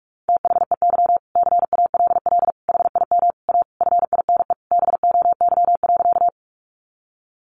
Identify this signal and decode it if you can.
Morse code, heard as a keyed tone, about 44 wpm, 700 Hz. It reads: THEY CALL HIM A FINE BOY.
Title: Sentences from Top 500 Words 45wpm